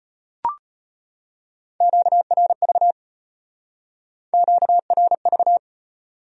Morse code, heard as a keyed tone, about 33 wpm, 700 Hz. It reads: QRV QRV